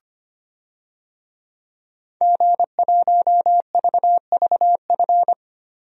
Morse code, heard as a keyed tone, about 25 wpm, 700 Hz. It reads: G1VVF
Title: DXCC-100 Call Signs - 5 Characters - 25wpm